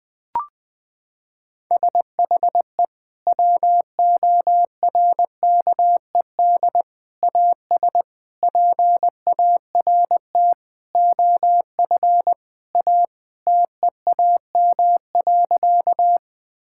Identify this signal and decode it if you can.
Morse code, heard as a keyed tone, about 20 wpm, 700 Hz. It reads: SHE WORKED AS PART OF A TEAM.
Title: Sentences with 4-Word Phrases - 20wpm